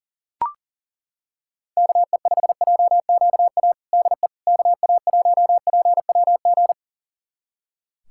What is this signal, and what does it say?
Morse code, heard as a keyed tone, about 37 wpm, 700 Hz. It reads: KE5JQA DE KA1PWG
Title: USA Callsign DE Callsign 40wpm